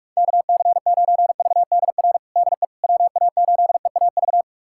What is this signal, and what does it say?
Morse code, heard as a keyed tone, about 46 wpm, 700 Hz. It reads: KC9VDR DE WA8EAV
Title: USA Callsign DE Callsign 45wpm